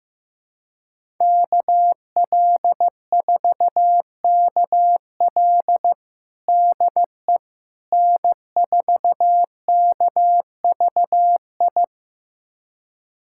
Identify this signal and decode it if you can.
Morse code, heard as a keyed tone, about 15 wpm, 700 Hz. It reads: KL4KL DE N4KVI
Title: USA Callsign DE Callsign - 5 Characters - 15wpm